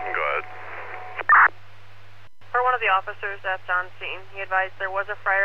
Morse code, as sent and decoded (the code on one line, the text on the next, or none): none